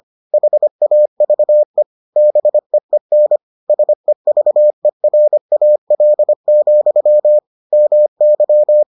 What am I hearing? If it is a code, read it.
Morse code, heard as a keyed tone, about 25 wpm, 600 Hz. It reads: HAVE BEEN SEVERAL, MY